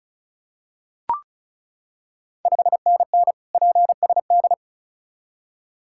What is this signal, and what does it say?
Morse code, heard as a keyed tone, about 36 wpm, 700 Hz. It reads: 5NN PSD
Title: Ontario QSO Party Exchange 35wpm